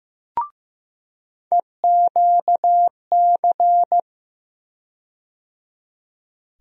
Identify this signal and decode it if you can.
Morse code, heard as a keyed tone, about 15 wpm, 700 Hz. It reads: EQC
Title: USA Call Sign Postfixes - 3 Letters 15wpm